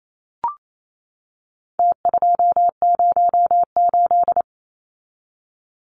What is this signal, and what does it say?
Morse code, heard as a keyed tone, about 28 wpm, 700 Hz. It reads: T208
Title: Numbers - Contest Style - 4-Digit 28wpm